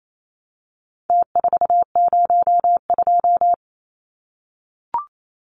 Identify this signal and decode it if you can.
Morse code, heard as a keyed tone, about 28 wpm, 700 Hz. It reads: T402
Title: Numbers - Contest Style - 4-Digit 28wpm